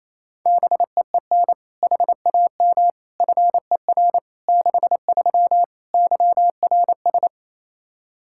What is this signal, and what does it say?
Morse code, heard as a keyed tone, about 28 wpm, 700 Hz. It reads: BEEN HAM FER 63 YRS
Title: Practice QSO - Exchange 3 - More Info - 28wpm